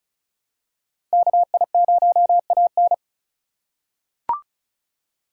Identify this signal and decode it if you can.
Morse code, heard as a keyed tone, about 34 wpm, 700 Hz. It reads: KI0AN
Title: USA Call Signs - 5 Characters - 2x Character Spacing - 35wpm